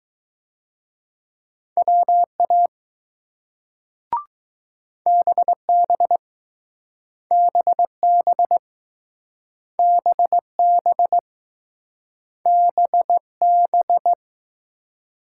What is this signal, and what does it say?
Morse code, heard as a keyed tone, about 20 wpm, 700 Hz. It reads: WA BB BB BB BB